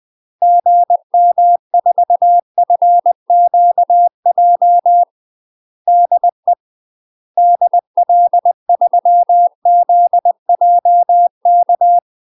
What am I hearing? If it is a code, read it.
Morse code, heard as a keyed tone, about 20 wpm, 700 Hz. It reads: GM4FQJ DE DL3ZJK
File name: 20WPM